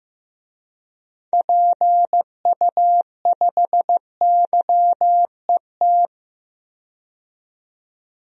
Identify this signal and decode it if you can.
Morse code, heard as a keyed tone, about 16 wpm, 700 Hz. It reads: PU5YET
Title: DXCC-100 Call Signs - 6 Characters - 15wpm